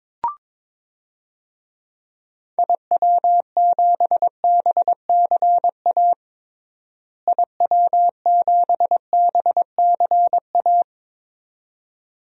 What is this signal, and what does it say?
Morse code, heard as a keyed tone, about 22 wpm, 700 Hz. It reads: IW7BCA IW7BCA
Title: DXCC-100 Call Signs - 6 Characters - Repeated 2x - 22wpm